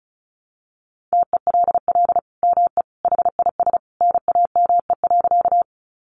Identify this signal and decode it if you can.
Morse code, heard as a keyed tone, about 34 wpm, 700 Hz. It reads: TELL ME HIS NAME.
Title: Sentences from Top 200 Words 35wpm